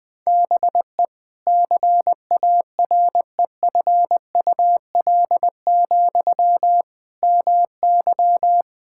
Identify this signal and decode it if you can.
Morse code, heard as a keyed tone, about 20 wpm, 700 Hz. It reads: BE CAREFUL, MY
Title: Sentences from Top 1600 Words 20wpm